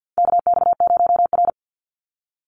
Morse code, heard as a keyed tone, about 55 wpm, 700 Hz. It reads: KX0R
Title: NA SOTA Activator Callsigns 50wpm